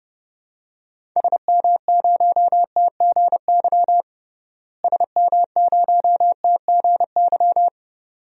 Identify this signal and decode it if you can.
Morse code, heard as a keyed tone, about 30 wpm, 700 Hz. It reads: SM0TGY SM0TGY
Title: DXCC-100 Call Signs - 6 Characters - Repeated 2x - 30wpm